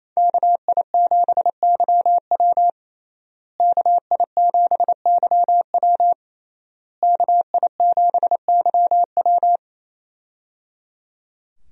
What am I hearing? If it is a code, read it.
Morse code, heard as a keyed tone, about 28 wpm, 700 Hz. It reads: KI7YW KI7YW KI7YW
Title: USA Call Signs - 5 Characters - Repeated 3x - 28wpm